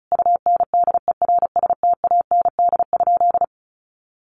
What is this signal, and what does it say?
Morse code, heard as a keyed tone, about 34 wpm, 700 Hz. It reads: UNDERSTAND?
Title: Sentences from Top 1500 Words 35wpm